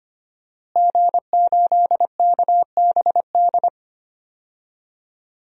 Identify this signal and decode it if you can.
Morse code, heard as a keyed tone, about 25 wpm, 700 Hz. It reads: G8KBD
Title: DXCC-100 Call Signs - 5 Characters - Repeated 2x - 25wpm